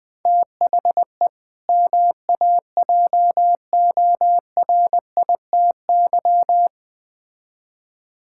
Morse code, heard as a keyed tone, about 20 wpm, 700 Hz. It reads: THE MAJORITY
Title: Build A Phrase - 3-Word Phrases - 20wpm